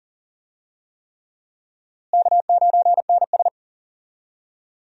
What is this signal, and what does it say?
Morse code, heard as a keyed tone, about 40 wpm, 700 Hz. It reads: K9NS
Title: USA Call Signs - 4 Characters - Repeated 3x - 40wpm